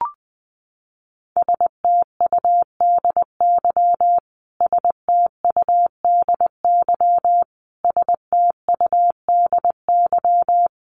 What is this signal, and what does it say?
Morse code, heard as a keyed tone, about 20 wpm, 700 Hz. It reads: STUDY STUDY STUDY
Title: Top 500 Words Repeated 20wpm